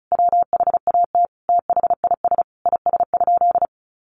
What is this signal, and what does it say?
Morse code, heard as a keyed tone, about 34 wpm, 700 Hz. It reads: WHAT THIS IS?
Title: Sentences from Top 1500 Words 35wpm